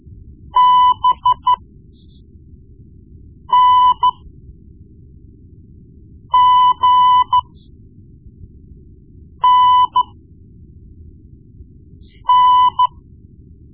Morse code, heard as a keyed tone, about 13 wpm, spaced as for 4 wpm, 1000 Hz. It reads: BNGNN
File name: Morse Lessons / Morse Lessons / Tape-1B